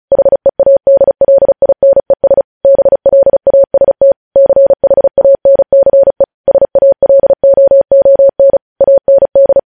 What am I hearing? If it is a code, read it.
Morse code, heard as a keyed tone, about 36 wpm, 550 Hz. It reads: HEADLINES BLAST CHANCE SALOON AND
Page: news headlines in morse code at 35 wpm